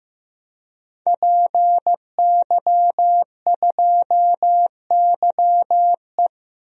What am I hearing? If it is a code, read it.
Morse code, heard as a keyed tone, about 15 wpm, 700 Hz. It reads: PY2YE
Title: DXCC-100 Call Signs - 5 Characters - Repeated 2x - 15wpm